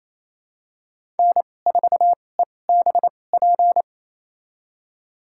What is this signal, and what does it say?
Morse code, heard as a keyed tone, about 28 wpm, spaced as for 20 wpm, 700 Hz. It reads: N4EBP
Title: USA Call Signs - 5 Characters - 2x Character Spacing - 28wpm